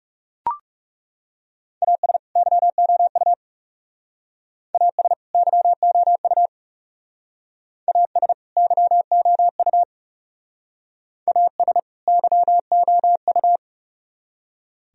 Morse code, heard as a keyed tone, about 37 wpm, 700 Hz. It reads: AS YOU AS YOU AS YOU AS YOU
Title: Speed-Racing - Sets of 2 Words 30wpm